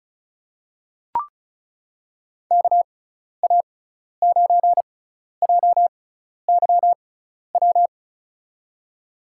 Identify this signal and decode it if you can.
Morse code, heard as a keyed tone, about 36 wpm, spaced as for 12 wpm, 690 Hz. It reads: KA9JYW
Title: USA Call Signs - 6 Characters - 6x Character Spacing - 35wpm